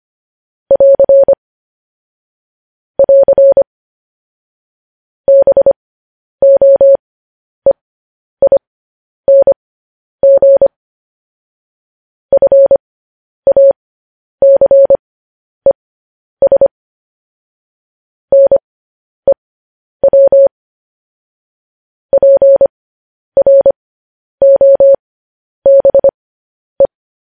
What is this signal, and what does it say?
Morse code, heard as a keyed tone, about 25 wpm, spaced as for 10 wpm, 550 Hz. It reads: <AR> <AR> BOEING FACES NEW PROBE
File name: podcast / News headlines-1x-25x10